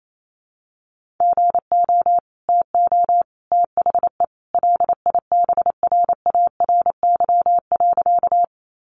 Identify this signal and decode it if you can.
Morse code, heard as a keyed tone, about 28 wpm, 700 Hz. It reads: GO TO THE LIBRARY.
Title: Sentences from Top 600 Words 28wpm